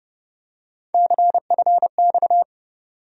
Morse code, heard as a keyed tone, about 30 wpm, 700 Hz. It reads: CFX
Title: USA Call Sign Postfixes - 3 Letters 30wpm